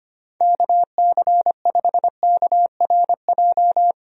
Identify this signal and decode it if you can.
Morse code, heard as a keyed tone, about 25 wpm, 700 Hz. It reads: KC5KRJ
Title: USA Call Signs - 6 Characters - 25wpm